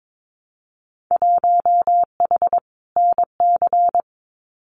Morse code, heard as a keyed tone, about 22 wpm, 700 Hz. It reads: 1H NC
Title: Winter Field Day Exchanges 22wpm